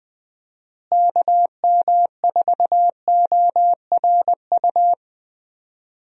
Morse code, heard as a keyed tone, about 20 wpm, 700 Hz. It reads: KM4ORU
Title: USA Call Signs - 6 Characters - Repeated 2x - 20wpm